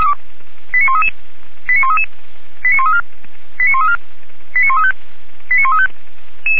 Five-tone selective call (ZVEI), heard as quick runs of five tones, tones of about 70 ms. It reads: 89133 89133 89134 89134 89135 89135